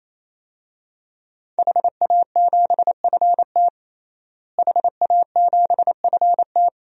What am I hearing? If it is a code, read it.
Morse code, heard as a keyed tone, about 28 wpm, 700 Hz. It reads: HA7FT HA7FT
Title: DXCC-100 Call Signs - 5 Characters - Repeated 3x - 28wpm